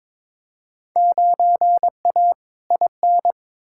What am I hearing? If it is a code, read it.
Morse code, heard as a keyed tone, about 22 wpm, 700 Hz. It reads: 9A IN
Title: Field Day Exchanges Encore - 22wpm